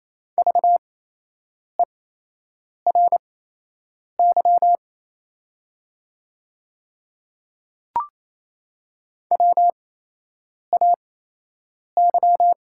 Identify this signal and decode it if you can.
Morse code, heard as a keyed tone, about 27 wpm, spaced as for 8 wpm, 690 Hz. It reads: VERY WAY
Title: 8x Character Spacing - Top 200 Words - 28wpm